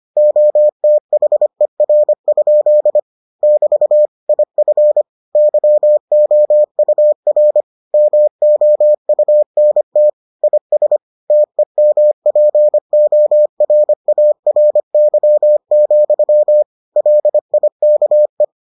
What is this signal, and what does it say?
Morse code, heard as a keyed tone, about 25 wpm, 600 Hz. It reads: OTHER? <BT> IF YOUR MOUNT IS TEMPORARY, LIKE